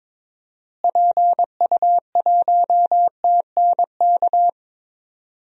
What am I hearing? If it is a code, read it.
Morse code, heard as a keyed tone, about 21 wpm, 700 Hz. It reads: PU1TNK
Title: DXCC-100 Call Signs - 6 Characters - Repeated 3x - 22wpm